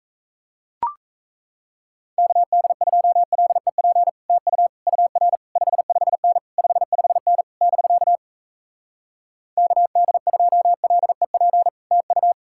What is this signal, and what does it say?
Morse code, heard as a keyed tone, about 38 wpm, 700 Hz. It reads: KD2LEP TU UR 55N 55N <BK> KD2LEP TU